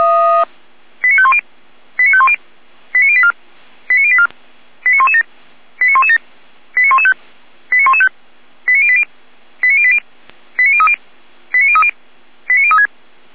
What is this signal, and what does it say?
Five-tone selective call (ZVEI), heard as quick runs of five tones, tones of about 70 ms. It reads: 89510 89510 89084 89084 89117 89117 89115 89115 89080 89080 89030 89030 89036
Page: SDR/ZVEI